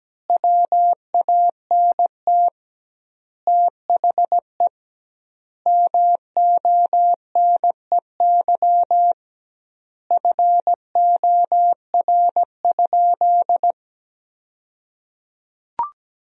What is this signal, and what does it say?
Morse code, heard as a keyed tone, about 17 wpm, 700 Hz. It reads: WANT THE MONEY FOR?
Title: 2x Word-Spacing - Sentences from Top 400 Words - 17wpm